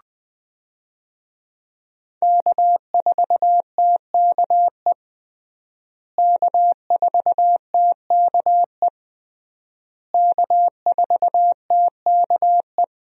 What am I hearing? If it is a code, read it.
Morse code, heard as a keyed tone, about 21 wpm, 700 Hz. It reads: K4TKE K4TKE K4TKE
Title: USA Call Signs - 5 Characters - Repeated 3x - 20wpm